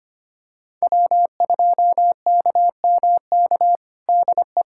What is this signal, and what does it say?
Morse code, heard as a keyed tone, about 25 wpm, 700 Hz. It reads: W2KMK DE
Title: USA Callsign DE Callsign - 5 Characters - 25wpm